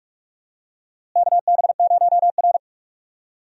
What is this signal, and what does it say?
Morse code, heard as a keyed tone, about 47 wpm, 700 Hz. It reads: KB0R
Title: USA Call Signs - 4 Characters - Repeated 3x - 45wpm